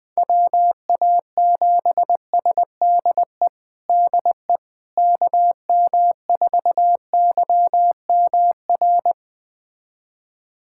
Morse code, heard as a keyed tone, about 20 wpm, 700 Hz. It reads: WA7SDE DE KM4YMR
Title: USA Callsign DE Callsign - 6 Characters - 20wpm